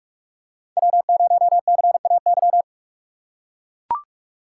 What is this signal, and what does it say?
Morse code, heard as a keyed tone, about 41 wpm, 700 Hz. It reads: W0CAY